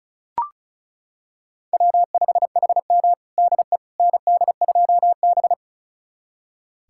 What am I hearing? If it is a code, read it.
Morse code, heard as a keyed tone, about 34 wpm, 700 Hz. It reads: W5HM DE ND2B
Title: USA Callsign DE Callsign - 4 Characters - 35wpm